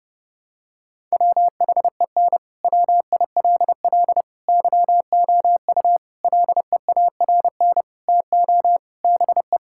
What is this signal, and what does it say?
Morse code, heard as a keyed tone, about 30 wpm, 700 Hz. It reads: WHEN WILL YOU LEARN TO BE
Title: Sentences from Top 1000 Words - Review Part 6 - 30wpm